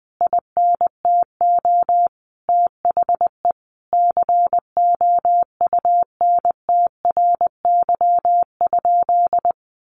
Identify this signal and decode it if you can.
Morse code, heard as a keyed tone, about 20 wpm, 700 Hz. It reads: INTO THE COUNTRY?
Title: Sentences from Top 1400 Words 20wpm